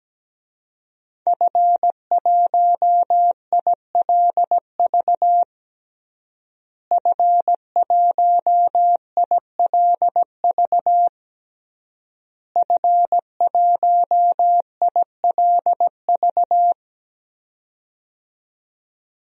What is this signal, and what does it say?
Morse code, heard as a keyed tone, about 17 wpm, 700 Hz. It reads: F1ILV F1ILV F1ILV
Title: DXCC-100 Call Signs - 5 Characters - Repeated 3x - 17wpm